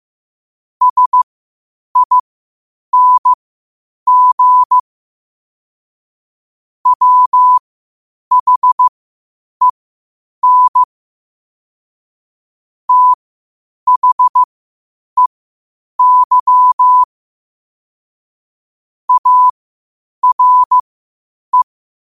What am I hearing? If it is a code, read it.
Morse code, heard as a keyed tone, about 15 wpm, spaced as for 8 wpm, 1000 Hz. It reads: SING WHEN THEY ARE
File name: qotdfiles / 8wpm